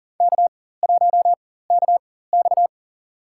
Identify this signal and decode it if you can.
Morse code, heard as a keyed tone, about 40 wpm, spaced as for 19 wpm, 700 Hz. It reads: K1KX
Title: USA Call Signs - 4 Characters - 4x Character Spacing - 40wpm